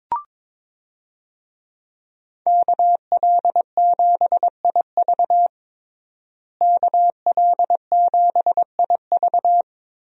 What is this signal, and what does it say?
Morse code, heard as a keyed tone, about 22 wpm, 700 Hz. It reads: KL7IV KL7IV
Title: USA Call Signs - 5 Characters - Repeated 3x - 22wpm